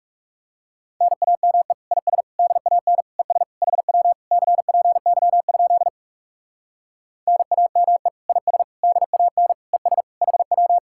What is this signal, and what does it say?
Morse code, heard as a keyed tone, about 40 wpm, 700 Hz. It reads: NAME IS DAN ES HW CPY? NAME IS DAN ES HW